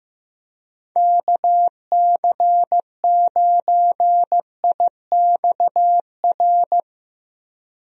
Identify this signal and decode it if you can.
Morse code, heard as a keyed tone, about 15 wpm, 700 Hz. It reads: KC9IXR
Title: USA Call Signs - 6 Characters - Repeated 3x - 15wpm